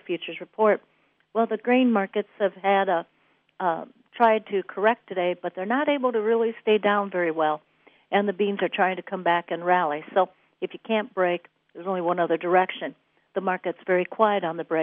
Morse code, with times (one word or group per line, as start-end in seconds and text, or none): none